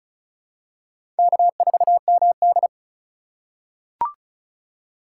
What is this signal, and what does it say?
Morse code, heard as a keyed tone, about 36 wpm, 700 Hz. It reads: K4MD
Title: USA Call Signs - 4 Characters - 3x Character Spacing - 35wpm